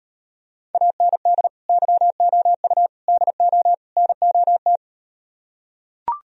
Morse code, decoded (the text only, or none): AND YOU DO NOT